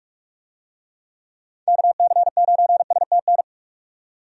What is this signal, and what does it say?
Morse code, heard as a keyed tone, about 46 wpm, 700 Hz. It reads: KC9STN
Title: USA Call Signs - 6 Characters - 45wpm